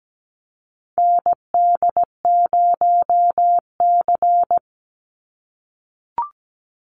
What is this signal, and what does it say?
Morse code, heard as a keyed tone, about 17 wpm, 700 Hz. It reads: ND0C
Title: NA SOTA Chaser Callsigns 17wpm